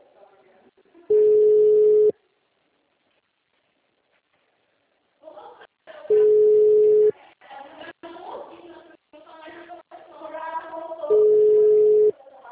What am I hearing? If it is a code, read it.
Ringback tone (Europe, 425 Hz), with speech.
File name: out- -2005- - - .143